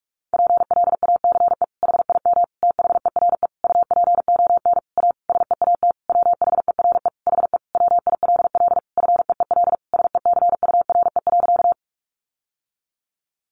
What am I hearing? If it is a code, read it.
Morse code, heard as a keyed tone, about 44 wpm, 700 Hz. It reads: PLACE HIM THERE UPON A SEAT WHERE HE WILL FEEL SECURE.
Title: Sentences from Top 1500 Words 45wpm